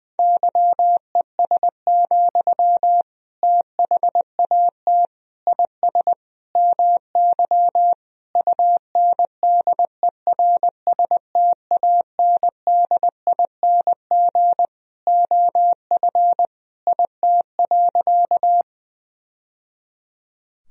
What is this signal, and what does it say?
Morse code, heard as a keyed tone, about 20 wpm, 700 Hz. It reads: YES, THAT IS MY UNDERSTANDING OF IT.
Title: Sentences from Top 1000 Words - Review Part 3 - 20wpm